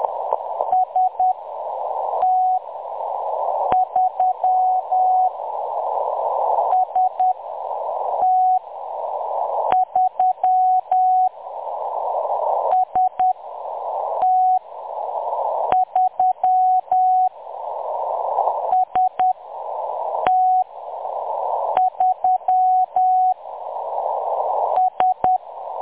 Morse code, heard as a keyed tone, about 10 wpm, spaced as for 6 wpm, 740 Hz. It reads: ST3 ST3 ST3 ST3 S
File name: M23-ST3- -0858utc-10184khz